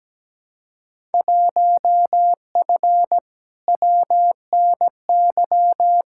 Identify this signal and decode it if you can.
Morse code, heard as a keyed tone, about 17 wpm, 700 Hz. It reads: 1F WNY
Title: Field Day Exchanges - 17wpm